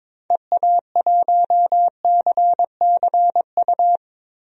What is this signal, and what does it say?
Morse code, heard as a keyed tone, about 22 wpm, 700 Hz. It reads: EA1CCU
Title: DXCC-100 Call Signs - 6 Characters - Repeated 3x - 22wpm